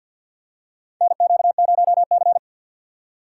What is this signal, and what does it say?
Morse code, heard as a keyed tone, about 52 wpm, 700 Hz. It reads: NQ0C
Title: USA Call Signs - 4 Characters - 6x Character Spacing - 50wpm